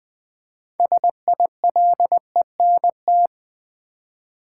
Morse code, heard as a keyed tone, about 20 wpm, 700 Hz. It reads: SILENT
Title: 6 Letter Words - First 3 Letters Given - 20wpm